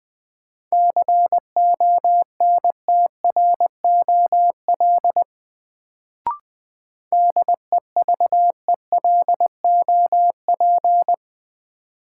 Morse code, heard as a keyed tone, about 20 wpm, 700 Hz. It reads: CONTROL DEVELOP
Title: Sentences from Top 1000 Words - Review Part 2 - New Word Drill - 20wpm